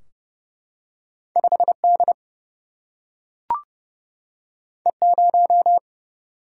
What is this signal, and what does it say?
Morse code, heard as a keyed tone, about 29 wpm, 700 Hz. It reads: 5 D E 0
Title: Single-Letter Single-Number 30wpm